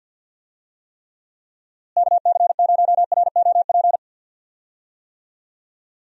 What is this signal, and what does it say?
Morse code, heard as a keyed tone, about 51 wpm, 700 Hz. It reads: KC0ROP
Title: USA Call Signs - 6 Characters - Repeated 2x - 50wpm